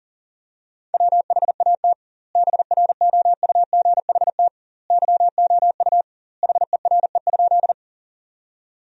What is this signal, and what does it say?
Morse code, heard as a keyed tone, about 40 wpm, 700 Hz. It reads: WHAT BROUGHT YOU HERE?
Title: 2x Word-Spacing - Sentences from Top 500 Words - 40wpm